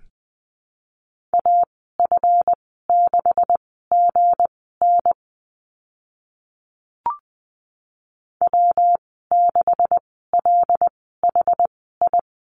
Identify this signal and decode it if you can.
Morse code, heard as a keyed tone, about 20 wpm, spaced as for 14 wpm, 700 Hz. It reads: AF6GN W6LHI
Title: 179 Lesson 29g - Slash - Call Signs 20wpm